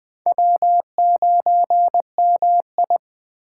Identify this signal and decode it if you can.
Morse code, heard as a keyed tone, about 20 wpm, 700 Hz. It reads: W9MI
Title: USA Call Signs - 4 Characters - 3x Character Spacing - 20wpm